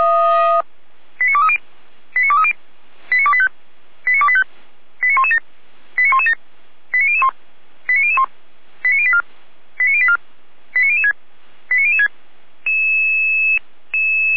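Five-tone selective call (ZVEI), heard as quick runs of five tones, tones of about 70 ms. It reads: 89230 89230 89285 89285 89117 89117 89001 89001 89084 89084 89006 89006
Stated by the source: ZVEI five-tone call received on SDR.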